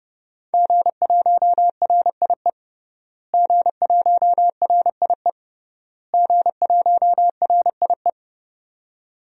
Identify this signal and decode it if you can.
Morse code, heard as a keyed tone, about 30 wpm, 700 Hz. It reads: G1RIE G1RIE G1RIE
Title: DXCC-100 Call Signs - 5 Characters - Repeated 3x - 30wpm